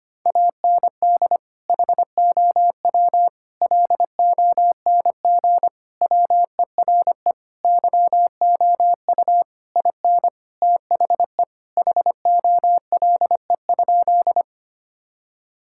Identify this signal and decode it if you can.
Morse code, heard as a keyed tone, about 25 wpm, 700 Hz. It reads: AND HOW LONG WERE YOU IN THE HOLE?
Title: Sentences from Top 1600 Words 25wpm